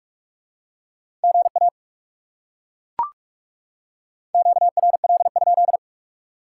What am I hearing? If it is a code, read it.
Morse code, heard as a keyed tone, about 38 wpm, 700 Hz. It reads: GA QRL?